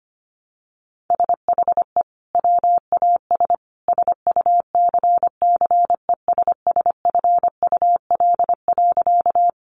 Morse code, heard as a keyed tone, about 25 wpm, 700 Hz. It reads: SHE WAS SUCCESSFUL.
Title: Sentences from Top 900 Words 25wpm